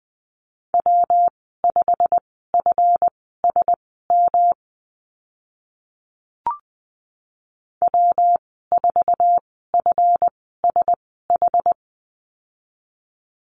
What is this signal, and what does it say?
Morse code, heard as a keyed tone, about 20 wpm, spaced as for 15 wpm, 700 Hz. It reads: W5FSM W4FSH
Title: 140 Lesson 23g - Letter f - Call Signs 20wpm